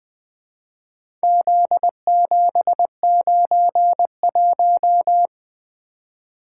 Morse code, heard as a keyed tone, about 23 wpm, 700 Hz. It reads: Z791
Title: Speed-Racing - Four Letter-Number Combinations - 20wpm